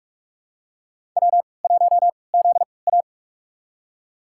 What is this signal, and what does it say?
Morse code, heard as a keyed tone, about 46 wpm, spaced as for 26 wpm, 700 Hz. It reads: W1ZA
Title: USA Call Signs - 3x Character Spacing - 45wpm